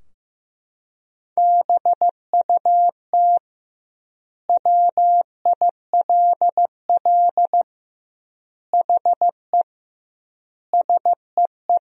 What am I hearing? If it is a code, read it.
Morse code, heard as a keyed tone, about 15 wpm, 700 Hz. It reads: BUT WILL HE SEE
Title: 2x Word-Spacing - Sentences from Top 100 Words - 15wpm